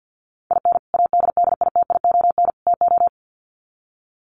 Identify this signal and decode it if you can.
Morse code, heard as a keyed tone, about 50 wpm, 700 Hz. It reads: IN ADDITION TO
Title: Sets of 3 Words 50wpm